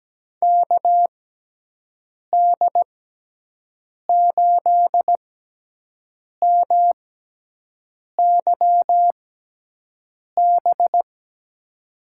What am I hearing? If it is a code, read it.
Morse code, heard as a keyed tone, about 17 wpm, spaced as for 6 wpm, 700 Hz. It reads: KD8MYB
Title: USA Call Signs - 6 Characters - 6x Character Spacing - 17wpm